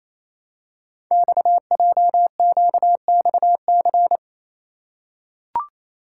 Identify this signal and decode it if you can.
Morse code, heard as a keyed tone, about 27 wpm, 700 Hz. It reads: XJQXC
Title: Code Groups - 5 Letters - 28wpm